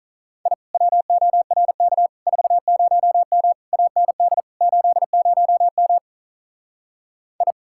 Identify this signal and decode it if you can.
Morse code, heard as a keyed tone, about 43 wpm, 700 Hz. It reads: I WORK 40M AND 80M I
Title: Speed-Racing Common QSO Phrases - 35wpm